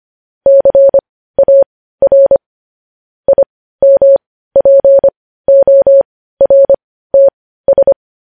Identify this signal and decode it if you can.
Morse code, heard as a keyed tone, about 25 wpm, spaced as for 15 wpm, 550 Hz. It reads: CAR IMPORTS
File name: podcast / News headlines-1x-25x15